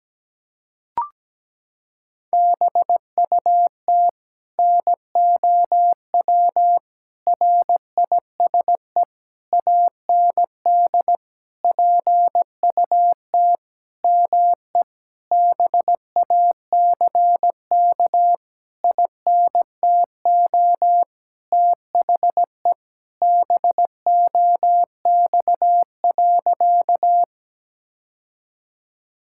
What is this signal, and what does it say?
Morse code, heard as a keyed tone, about 17 wpm, 700 Hz. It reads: BUT NOW RISE AND PUT ME BACK INTO THE BOX.
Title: Sentences from Top 1000 Words - Review Part 6 - 17wpm